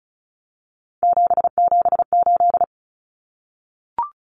Morse code, heard as a keyed tone, about 34 wpm, 700 Hz. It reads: 778
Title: Common Cut Numbers - 3-Digit 35wpm